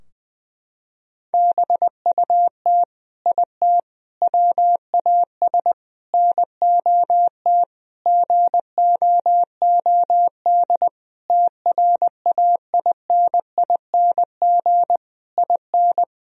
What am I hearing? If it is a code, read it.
Morse code, heard as a keyed tone, about 21 wpm, 700 Hz. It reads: BUT IT WAS NOT GOOD TRAINING IN
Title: Sentences from Top 1000 Words - Review Part 4 - 20wpm